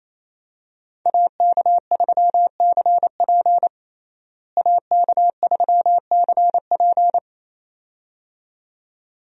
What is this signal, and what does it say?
Morse code, heard as a keyed tone, about 28 wpm, 700 Hz. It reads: AK3CP AK3CP
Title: USA Call Signs - 5 Characters - Repeated 2x - 28wpm